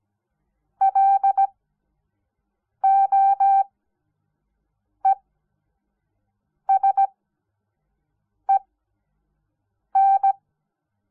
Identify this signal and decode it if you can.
Morse code, heard as a keyed tone, about 17 wpm, spaced as for 5 wpm, 780 Hz. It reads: LOESEN
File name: Onlinelehrgang / morsen / DJ4UF Morsen06